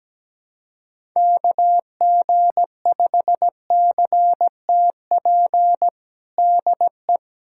Morse code, heard as a keyed tone, about 17 wpm, 700 Hz. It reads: KG5CTP DE
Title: USA Callsign DE Callsign - 6 Characters - 17wpm